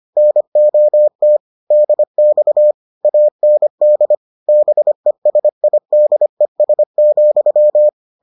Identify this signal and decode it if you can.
Morse code, heard as a keyed tone, about 25 wpm, 600 Hz. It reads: NOT DX AND BESIDES,